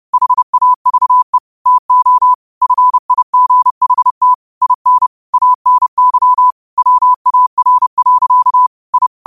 Morse code, heard as a keyed tone, about 30 wpm, 1000 Hz. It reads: HAVE TO FIGHT IN ANY WAR. I
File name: qotdfiles / 30wpm